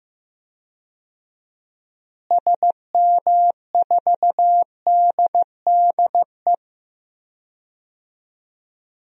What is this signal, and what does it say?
Morse code, heard as a keyed tone, about 15 wpm, 700 Hz. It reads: SM4DDE
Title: DXCC-100 Call Signs - 6 Characters - Repeated 2x - 15wpm